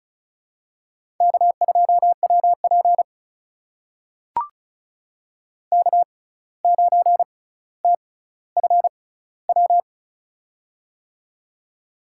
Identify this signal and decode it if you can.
Morse code, heard as a keyed tone, about 36 wpm, spaced as for 17 wpm, 700 Hz. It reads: K2WP K9TFW
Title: USA Call Signs - 6x Character Spacing - 35wpm